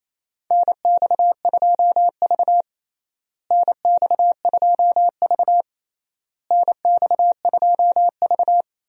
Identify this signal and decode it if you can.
Morse code, heard as a keyed tone, about 28 wpm, 700 Hz. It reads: NX2V NX2V NX2V
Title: USA Call Signs - 4 Characters - Repeated 3x - 28wpm